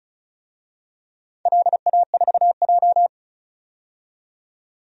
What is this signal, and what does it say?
Morse code, heard as a keyed tone, about 36 wpm, 690 Hz. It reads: LA4J
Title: DXCC-100 Call Signs - 4 Characters - 35wpm